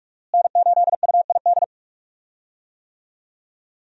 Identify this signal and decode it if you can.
Morse code, heard as a keyed tone, about 43 wpm, 700 Hz. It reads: N8UID
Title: USA Call Signs - 5 Characters - Repeated 2x - 45wpm